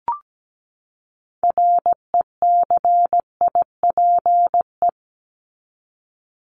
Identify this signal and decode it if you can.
Morse code, heard as a keyed tone, about 17 wpm, 700 Hz. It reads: RECIPE
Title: Top 1001-1250 Words 17wpm